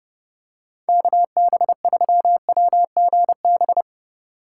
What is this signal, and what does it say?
Morse code, heard as a keyed tone, about 32 wpm, 700 Hz. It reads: KB3WGB
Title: USA Call Signs - 6 Characters - 8x Character Spacing - 30wpm